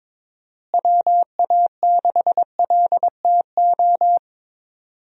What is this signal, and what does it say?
Morse code, heard as a keyed tone, about 22 wpm, 700 Hz. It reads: WA6LTO
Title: USA Call Signs - 6 Characters - Repeated 2x - 22wpm